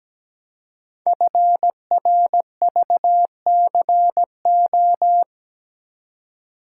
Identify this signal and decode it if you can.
Morse code, heard as a keyed tone, about 17 wpm, 700 Hz. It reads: FRVCO
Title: Code Groups - 5 Letters - 17wpm